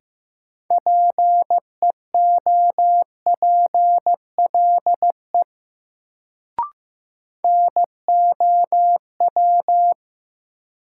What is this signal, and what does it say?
Morse code, heard as a keyed tone, about 15 wpm, 700 Hz. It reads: PEOPLE NOW
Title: Sets of 2 Words - New Word Drill - 15wpm